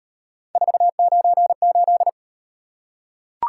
Morse code, heard as a keyed tone, about 38 wpm, 700 Hz. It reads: 498